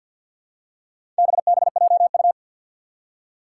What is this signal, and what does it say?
Morse code, heard as a keyed tone, about 51 wpm, 700 Hz. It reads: BBJU
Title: Four Challenging Letter-Number Combinations 50wpm